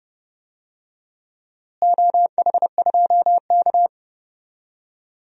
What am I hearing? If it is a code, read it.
Morse code, heard as a keyed tone, about 32 wpm, 700 Hz. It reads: OH2K
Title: DXCC-100 Call Signs - 4 Characters - Repeated 2x - 30wpm